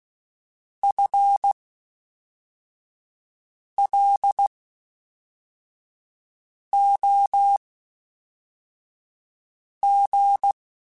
Morse code, heard as a keyed tone, about 16 wpm, spaced as for 4 wpm, 800 Hz. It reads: FLOG
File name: Spaced / FL Runs sp mp3s / FL Runs sp @5